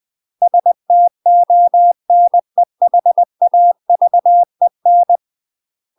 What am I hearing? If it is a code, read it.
Morse code, heard as a keyed tone, about 20 wpm, 700 Hz. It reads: STONEHAVEN
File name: UK MASTER QSOs TRAINING / 20WPM / UK MASTER QSOs 0015